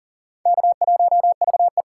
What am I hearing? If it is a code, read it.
Morse code, heard as a keyed tone, about 40 wpm, 700 Hz. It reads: K1VE
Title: USA Call Signs - 3x Character Spacing - 40wpm